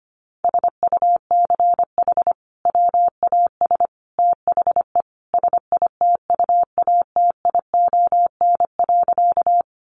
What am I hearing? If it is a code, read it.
Morse code, heard as a keyed tone, about 25 wpm, 700 Hz. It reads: SUCH WAS THE SITUATION.
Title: Sentences from Top 1200 Words 25wpm